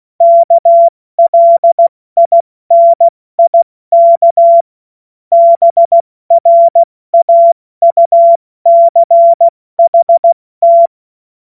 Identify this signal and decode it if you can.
Morse code, heard as a keyed tone, about 16 wpm, 680 Hz. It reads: KLINIK BRAUCHT
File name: Morsekurs / Witze / Witz 06. September 2022 70 F80